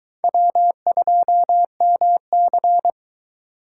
Morse code, heard as a keyed tone, about 23 wpm, 700 Hz. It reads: W2MC